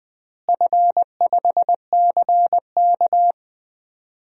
Morse code, heard as a keyed tone, about 20 wpm, 700 Hz. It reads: F5CK
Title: DXCC-100 Call Signs - 4 Characters - Repeated 2x - 20wpm